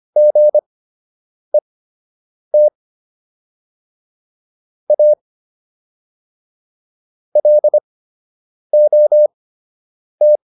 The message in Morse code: GET A LOT

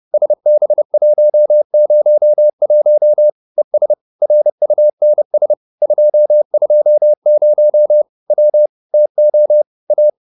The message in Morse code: SB101 ES RUNS 220 W TO A

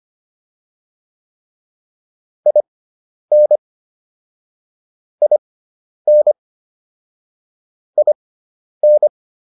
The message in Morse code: IN IN IN

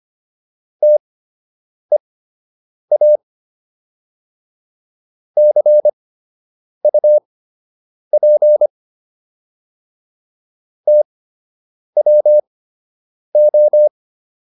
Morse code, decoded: TEA CUP TWO

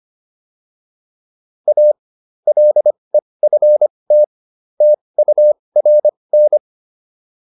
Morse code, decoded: A LEFT TURN

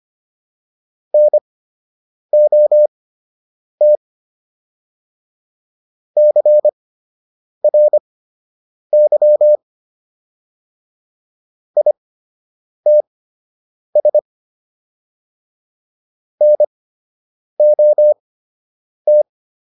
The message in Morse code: NOT CRY ITS NOT

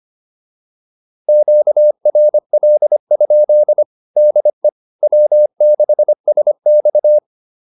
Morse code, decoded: QRL? DE W6SX

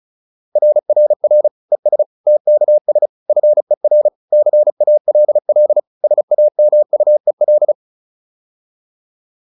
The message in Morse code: RRR ES TKS FER CALL SAMUEL

